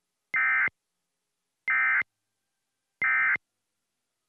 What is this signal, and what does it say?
Emergency Alert System (SAME) tones, heard as harsh buzzing bursts. It is heard from 0.3 to 3.4 s.